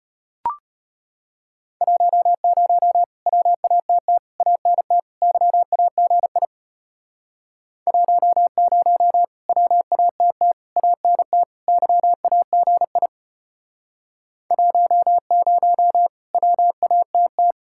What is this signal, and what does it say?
Morse code, heard as a keyed tone, about 33 wpm, 700 Hz. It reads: 10 WATT ANT YAGI 10 WATT ANT YAGI 10 WATT